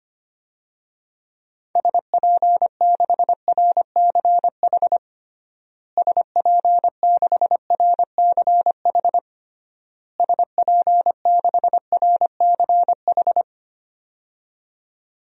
Morse code, heard as a keyed tone, about 25 wpm, 700 Hz. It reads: SP6RCH SP6RCH SP6RCH
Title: DXCC-100 Call Signs - 6 Characters - Repeated 3x - 25wpm